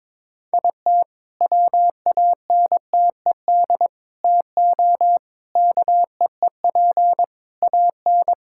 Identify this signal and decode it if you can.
Morse code, heard as a keyed tone, about 21 wpm, 690 Hz. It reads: IT WANTED TO KEEP AN
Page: morse-code-ninja-practice-22wpm